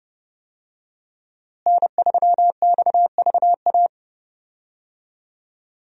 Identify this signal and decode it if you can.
Morse code, heard as a keyed tone, about 32 wpm, 700 Hz. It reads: N3XVA
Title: USA Call Signs - 5 Characters - Repeated 3x - 30wpm